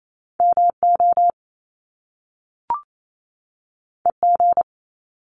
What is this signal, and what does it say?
Morse code, heard as a keyed tone, about 28 wpm, 700 Hz. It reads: MO EG
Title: Top Two Letter Combinations in Words 28wpm